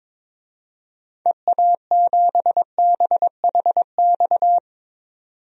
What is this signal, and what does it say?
Morse code, heard as a keyed tone, about 22 wpm, 700 Hz. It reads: EA7BHX
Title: DXCC-100 Call Signs - 6 Characters - 22wpm